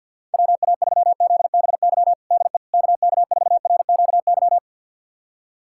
Morse code, heard as a keyed tone, about 50 wpm, 700 Hz. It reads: WA3ZBY DE KK4RQY
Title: USA Callsign DE Callsign 50wpm